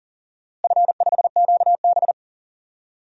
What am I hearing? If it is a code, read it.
Morse code, heard as a keyed tone, about 40 wpm, 700 Hz. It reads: F5QB
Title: DXCC-100 Call Signs - 4 Characters - Repeated 3x - 40wpm